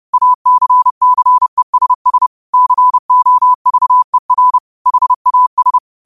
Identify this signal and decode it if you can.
Morse code, heard as a keyed tone, about 30 wpm, 1000 Hz. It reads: ACCESS COVER HAS
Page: New quotes every day in morse code at 30 Words per minute.